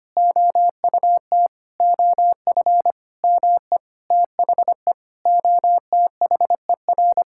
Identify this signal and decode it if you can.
Morse code, heard as a keyed tone, about 25 wpm, 700 Hz. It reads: OUT OF ME THE OTHER
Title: Sentences from Top 1000 Words - Review Part 5 - 25wpm